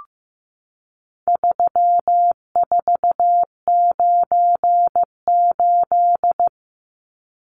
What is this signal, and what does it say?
Morse code, heard as a keyed tone, about 15 wpm, 700 Hz. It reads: 3498
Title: Numbers - 4-Digit 15wpm